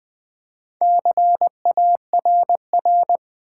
Morse code, heard as a keyed tone, about 20 wpm, 700 Hz. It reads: CARR
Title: Build A Word - 2 Letters At A Time - 8-Letter Words - 20wpm